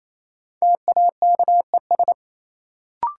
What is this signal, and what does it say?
Morse code, heard as a keyed tone, about 28 wpm, 700 Hz. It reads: TAKES
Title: Sets of 5 Words Encore - New Word Drill - 28wpm